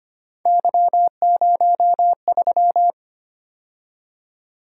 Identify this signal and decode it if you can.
Morse code, heard as a keyed tone, about 30 wpm, 700 Hz. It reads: Y03
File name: Speed-Racing - Three Letter-Number Combinations - 25wpm